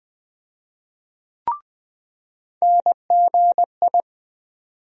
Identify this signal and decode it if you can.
Morse code, heard as a keyed tone, about 20 wpm, 700 Hz. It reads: NGI
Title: USA Call Sign Postfixes - 3 Letters 20wpm